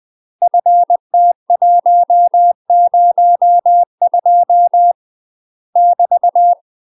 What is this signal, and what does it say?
Morse code, heard as a keyed tone, about 20 wpm, 700 Hz. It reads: FT102 <BT>
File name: UK MASTER QSOs TRAINING / 20WPM / UK MASTER QSOs 0008